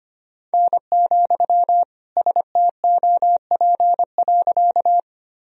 Morse code, heard as a keyed tone, about 25 wpm, 700 Hz. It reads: N, STOP.
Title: Sentences from Top 300 Words and Abbreviations 25wpm